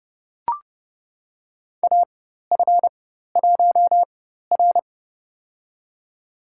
Morse code, heard as a keyed tone, about 30 wpm, spaced as for 14 wpm, 700 Hz. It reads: AF1R
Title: USA Call Signs - 4x Character Spacing - 30wpm